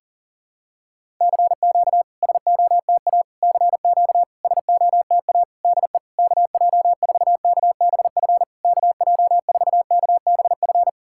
Morse code, heard as a keyed tone, about 40 wpm, 700 Hz. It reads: CQ SOTA CQ SOTA DE KJ4KBF KJ4KBF
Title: SOTA QSO Practice 40wpm